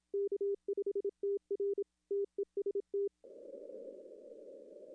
Morse code, heard as a keyed tone, about 27 wpm, 400 Hz. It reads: K5TR TEST
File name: contests / arrl-160-k5tr-op-w5tn -0600